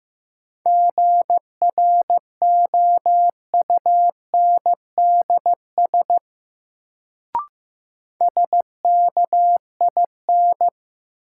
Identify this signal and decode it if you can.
Morse code, heard as a keyed tone, about 15 wpm, 700 Hz. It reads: GROUNDS SKIN
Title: Sentences from Top 1300 Words - New Word Drill Alt - 15wpm